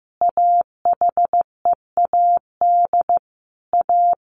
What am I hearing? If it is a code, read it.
Morse code, heard as a keyed tone, about 15 wpm, 700 Hz. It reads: AHEAD A
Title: Sentences from Top 1100 Words 15wpm